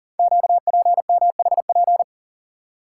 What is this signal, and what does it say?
Morse code, heard as a keyed tone, about 44 wpm, 690 Hz. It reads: QPMHP
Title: Code Groups - 5 Letters - 40wpm